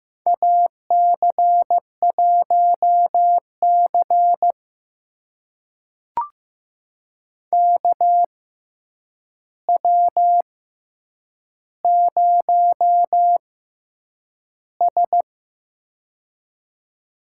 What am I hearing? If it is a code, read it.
Morse code, heard as a keyed tone, about 15 wpm, spaced as for 8 wpm, 700 Hz. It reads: AC1C KW0S
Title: USA Call Signs - 4 Characters - 6x Character Spacing - 15wpm